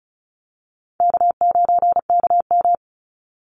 Morse code, heard as a keyed tone, about 34 wpm, 700 Hz. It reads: K9KM
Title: Call Signs Repeated 2x - 35wpm